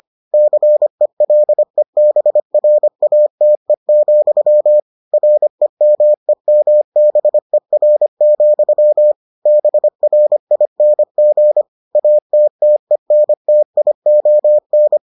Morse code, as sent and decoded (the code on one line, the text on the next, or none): -.-. .-.. -... .-. .- - . --..-- / .-. -- . -- -... .-. --..-- / -... .-. .. -. --. / .- - - . -. - .. --- -.
CELEBRATE, REMEMBER, BRING ATTENTION